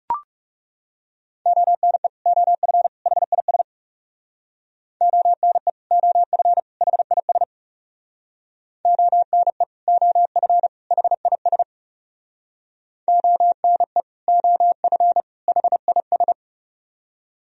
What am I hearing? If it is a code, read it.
Morse code, heard as a keyed tone, about 40 wpm, 700 Hz. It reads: ONE OF HIS ONE OF HIS ONE OF HIS ONE OF HIS